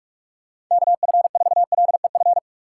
Morse code, heard as a keyed tone, about 49 wpm, 700 Hz. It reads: KF4LEF